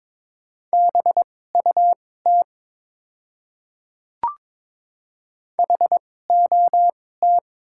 Morse code, heard as a keyed tone, about 22 wpm, spaced as for 16 wpm, 700 Hz. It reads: BUT HOT
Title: 2x Character Spacing - Top 100 Words - 22wpm